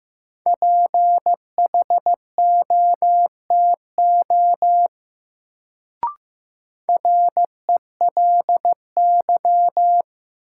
Morse code, heard as a keyed tone, about 15 wpm, 700 Hz. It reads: PHOTO RELY
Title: Sentences from Top 1200 Words - New Word Drill - 15wpm